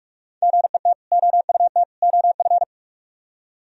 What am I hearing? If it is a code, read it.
Morse code, heard as a keyed tone, about 60 wpm, spaced as for 53 wpm, 700 Hz. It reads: GET OUT OF
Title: Speed-Racing - Sets of 3 Words 45wpm